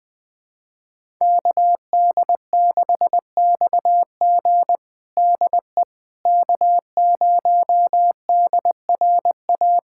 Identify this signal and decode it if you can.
Morse code, heard as a keyed tone, about 20 wpm, 700 Hz. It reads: KD6XG DE K0DRA
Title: USA Callsign DE Callsign - 5 Characters - 20wpm